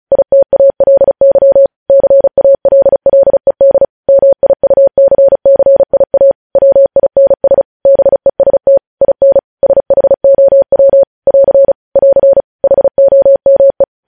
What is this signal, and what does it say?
Morse code, heard as a keyed tone, about 36 wpm, 550 Hz. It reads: ITALY CALLED MIUCCIA WINS BEST IN SHOW <AR> <AR> HOME
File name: podcast / News headlines-1x-35x35